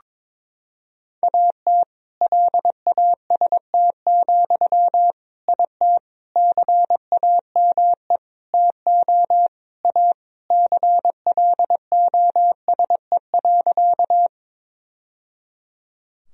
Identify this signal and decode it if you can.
Morse code, heard as a keyed tone, about 22 wpm, 700 Hz. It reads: AT LAST, IT CAME TO A CLOSE.
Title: Sentences from Top 1000 Words - Review Part 1 - 22wpm